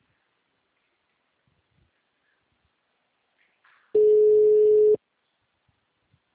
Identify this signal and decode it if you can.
Ringback tone (Europe, 425 Hz).
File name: out- -2014- - - .6951